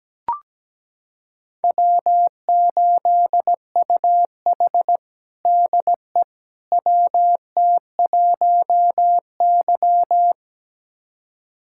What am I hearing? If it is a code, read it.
Morse code, heard as a keyed tone, about 17 wpm, 700 Hz. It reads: W8UH DE WT1Y
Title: USA Callsign DE Callsign - 4 Characters - 17wpm